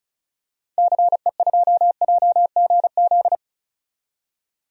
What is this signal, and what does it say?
Morse code, heard as a keyed tone, about 34 wpm, 700 Hz. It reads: CE2JGZ
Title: DXCC-100 Call Signs - 6 Characters - 35wpm